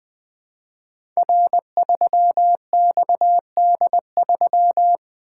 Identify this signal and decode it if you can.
Morse code, heard as a keyed tone, about 20 wpm, 700 Hz. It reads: R3XD3
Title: Code Groups - 5 Letter:Numbers - 20wpm